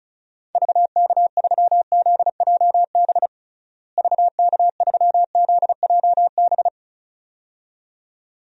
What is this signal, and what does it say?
Morse code, heard as a keyed tone, about 34 wpm, 700 Hz. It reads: VK3ZJB VK3ZJB
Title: DXCC-100 Call Signs - 6 Characters - Repeated 2x - 35wpm